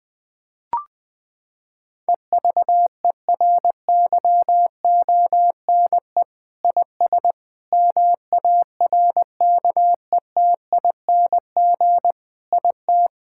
Morse code, heard as a keyed tone, about 20 wpm, 700 Hz. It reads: EVERYONE IS MARKETING IT
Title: Sentences from Top 1000 Words - Review Part 5 - 20wpm